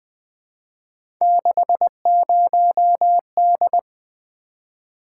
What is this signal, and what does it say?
Morse code, heard as a keyed tone, about 20 wpm, 700 Hz. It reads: 60D
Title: Three Letter-Number Combinations - 20wpm